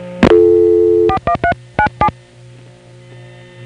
DTMF, heard as telephone touch tones, tones of about 80 ms. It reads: 41397